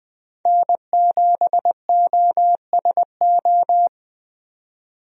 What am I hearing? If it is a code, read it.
Morse code, heard as a keyed tone, about 20 wpm, 700 Hz. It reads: N7OSO
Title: USA Call Signs - 5 Characters - 20wpm